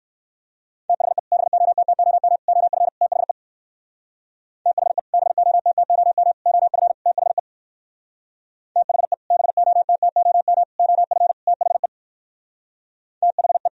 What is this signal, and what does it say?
Morse code, heard as a keyed tone, about 58 wpm, 700 Hz. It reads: THE BOTTOM OF THE THE BOTTOM OF THE THE BOTTOM OF THE THE